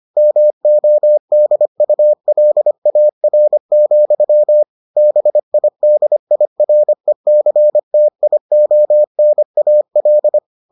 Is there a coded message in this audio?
Yes: Morse code from 0.2 to 10.4 s, heard as a keyed tone, about 25 wpm, 600 Hz. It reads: MODULAR, BIDIRECTIONAL